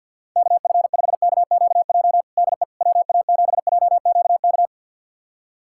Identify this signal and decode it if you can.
Morse code, heard as a keyed tone, about 52 wpm, 700 Hz. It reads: KF5KQJ DE WA7JQK
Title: USA Callsign DE Callsign 50wpm